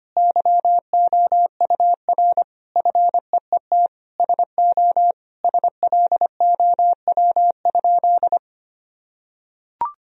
Morse code, heard as a keyed tone, about 25 wpm, 700 Hz. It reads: YOUR FEET SO SLOW?
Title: Sentences from Top 1000 Words - Review Part 2 - 25wpm